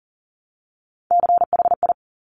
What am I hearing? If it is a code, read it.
Morse code, heard as a keyed tone, about 40 wpm, 690 Hz. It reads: CHI
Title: Top Three Letter Combinations in Words 40wpm